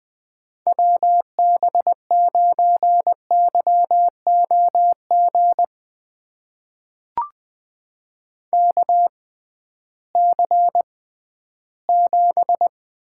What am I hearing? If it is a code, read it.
Morse code, heard as a keyed tone, about 20 wpm, spaced as for 13 wpm, 700 Hz. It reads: WB9YOG KC7
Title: USA Call Signs - 6 Characters - 6x Character Spacing - 20wpm